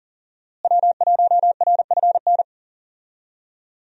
Morse code, heard as a keyed tone, about 40 wpm, 700 Hz. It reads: W1RFN
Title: USA Call Signs - 5 Characters - Repeated 3x - 40wpm